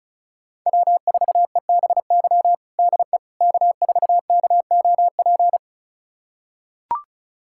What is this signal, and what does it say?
Morse code, heard as a keyed tone, about 36 wpm, 700 Hz. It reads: W4EBY DE K4KOP
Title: USA Callsign DE Callsign - 5 Characters - 35wpm